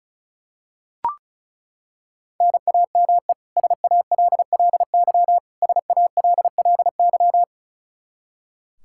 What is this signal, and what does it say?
Morse code, heard as a keyed tone, about 36 wpm, 700 Hz. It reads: NAME SALLY SALLY
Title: K1USN SST QSO Practice 35wpm